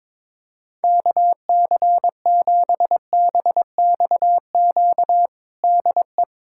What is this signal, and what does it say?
Morse code, heard as a keyed tone, about 22 wpm, 700 Hz. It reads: KC7BXQ DE
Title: USA Callsign DE Callsign 22wpm